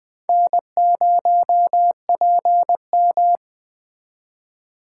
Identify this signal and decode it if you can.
Morse code, heard as a keyed tone, about 20 wpm, 700 Hz. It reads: N0PM
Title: USA Call Signs - 4 Characters - 3x Character Spacing - 20wpm